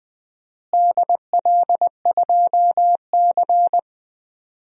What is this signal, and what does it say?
Morse code, heard as a keyed tone, about 20 wpm, 700 Hz. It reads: DL2C
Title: DXCC-100 Call Signs - 4 Characters - Repeated 2x - 20wpm